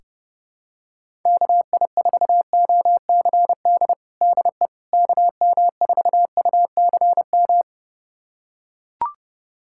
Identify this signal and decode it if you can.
Morse code, heard as a keyed tone, about 32 wpm, 690 Hz. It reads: KI4OCD DE KM4UCM
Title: USA Callsign DE Callsign - 6 Characters - 30wpm